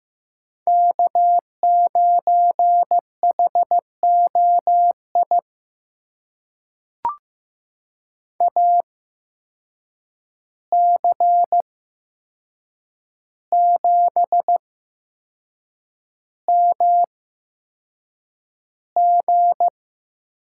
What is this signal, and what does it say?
Morse code, heard as a keyed tone, about 15 wpm, spaced as for 6 wpm, 700 Hz. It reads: K9HOI AC7MG
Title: USA Call Signs - 5 Characters - 8x Character Spacing - 15wpm